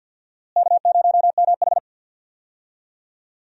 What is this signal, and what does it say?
Morse code, heard as a keyed tone, about 51 wpm, 700 Hz. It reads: K0MH
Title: USA Call Signs - 4 Characters - Repeated 3x - 50wpm